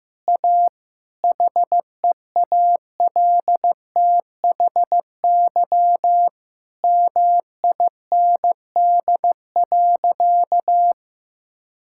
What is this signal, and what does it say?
Morse code, heard as a keyed tone, about 16 wpm, 700 Hz. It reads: A HEALTHY MIND.
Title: Sentences from Top 1000 Words - Review Part 5 - 15wpm